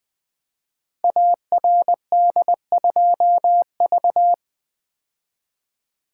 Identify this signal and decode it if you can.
Morse code, heard as a keyed tone, about 20 wpm, 700 Hz. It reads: ARD2V
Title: Code Groups - 5 Letter:Numbers - 20wpm